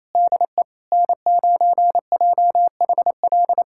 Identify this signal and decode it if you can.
Morse code, heard as a keyed tone, about 28 wpm, 700 Hz. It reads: DE N9JHL
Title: USA Callsign DE Callsign - 5 Characters - 28wpm